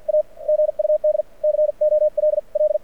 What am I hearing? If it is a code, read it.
Morse code, heard as a keyed tone, about 54 wpm, spaced as for 44 wpm, 610 Hz. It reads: A GUD COLL
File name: archived / CRSnet